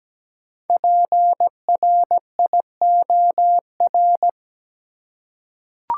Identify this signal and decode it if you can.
Morse code, heard as a keyed tone, about 17 wpm, 700 Hz. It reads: PRIOR
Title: Five Letter Words - 17wpm